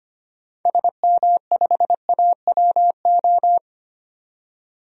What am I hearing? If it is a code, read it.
Morse code, heard as a keyed tone, about 25 wpm, 700 Hz. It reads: SM5AWO
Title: DXCC-100 Call Signs - 6 Characters - 25wpm